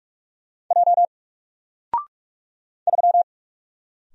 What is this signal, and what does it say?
Morse code, heard as a keyed tone, about 46 wpm, 700 Hz. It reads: J 3
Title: Single Letter-Number Alt 45wpm